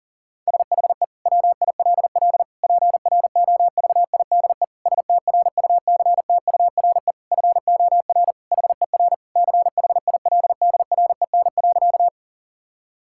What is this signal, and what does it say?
Morse code, heard as a keyed tone, about 40 wpm, 700 Hz. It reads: SHE WILL PROVIDE STRUCTURE FOR HER CHILDREN.
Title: Sentences from Top 1000 Words - Review Part 6 - 40wpm